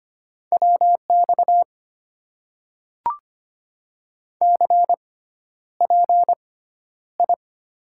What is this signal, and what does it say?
Morse code, heard as a keyed tone, about 25 wpm, spaced as for 11 wpm, 700 Hz. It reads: WX CPI
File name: Top 90 QSO Elements - 6x Character Spacing - 25wpm